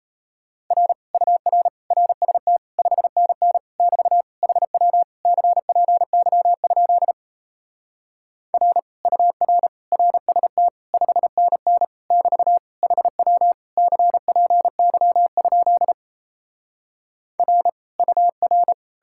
Morse code, heard as a keyed tone, about 37 wpm, 700 Hz. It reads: R UR RST 5NN <BT> HW CPY? R UR RST 5NN <BT> HW CPY? R UR